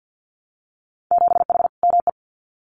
Morse code, heard as a keyed tone, about 50 wpm, 700 Hz. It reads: 7H ME
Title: Winter Field Day Exchanges Encore 50wpm